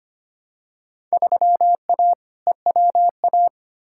Morse code, heard as a keyed tone, about 25 wpm, 700 Hz. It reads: 3A EWA
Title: Field Day Exchanges Encore - 25wpm